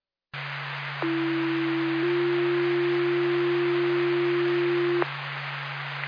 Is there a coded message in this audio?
Yes: two-tone page from 1.0 to 5.0 s, heard as a one-second tone, then a three-second tone, about 330 then 350 Hz, with noise.